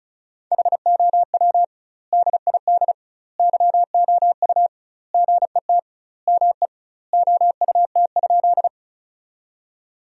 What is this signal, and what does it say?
Morse code, heard as a keyed tone, about 34 wpm, 700 Hz. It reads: HOW DID YOU GET ME OUT?
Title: 2x Word-Spacing - Sentences from Top 200 Words - 35wpm